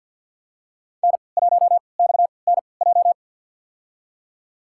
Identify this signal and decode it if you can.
Morse code, heard as a keyed tone, about 51 wpm, spaced as for 28 wpm, 700 Hz. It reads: N1XNJ
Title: USA Call Signs - 5 Characters - 3x Character Spacing - 50wpm